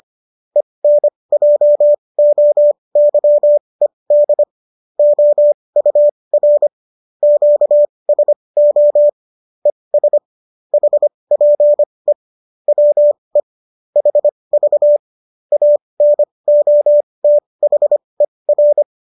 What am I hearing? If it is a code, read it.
Morse code, heard as a keyed tone, about 25 wpm, spaced as for 20 wpm, 590 Hz. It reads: ENJOYED OUR QSO ES HPE WE HV ANOTHER